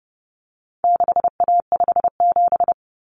Morse code, heard as a keyed tone, about 30 wpm, 700 Hz. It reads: 6A57
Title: Common Cut Numbers - 4-Digit 30wpm